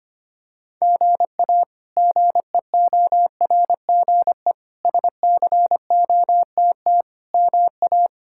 Morse code, heard as a keyed tone, about 25 wpm, 700 Hz. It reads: GA GEORGE SCOTT MA
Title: K1USN SST QSO Practice - 25wpm